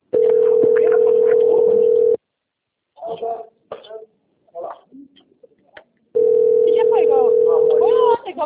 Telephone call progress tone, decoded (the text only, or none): ringback tone